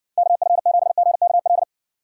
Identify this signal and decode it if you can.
Morse code, heard as a keyed tone, about 60 wpm, 700 Hz. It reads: KF7GKL